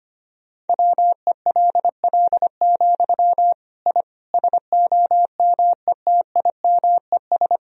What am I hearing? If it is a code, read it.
Morse code, heard as a keyed tone, about 25 wpm, 700 Hz. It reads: WELL, I SOMETIMES
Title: Sentences from Top 1000 Words - Review Part 3 - 25wpm